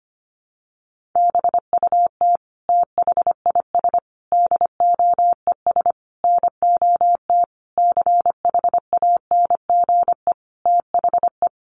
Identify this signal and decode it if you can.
Morse code, heard as a keyed tone, about 25 wpm, 700 Hz. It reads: BUT THIS DOES NOT CHANGE THE
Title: Sentences from Top 1400 Words 25wpm